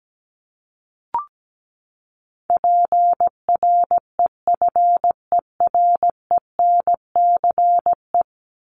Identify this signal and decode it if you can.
Morse code, heard as a keyed tone, about 17 wpm, 700 Hz. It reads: PREFERENCE
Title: Top 1251-1500 Words 17wpm